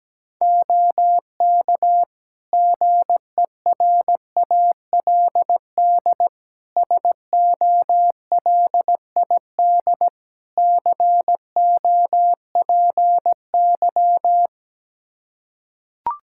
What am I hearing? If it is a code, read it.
Morse code, heard as a keyed tone, about 17 wpm, 700 Hz. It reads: OK GERALD SOLID COPY
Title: Practice QSO - Exchange 3 - More Info - 17wpm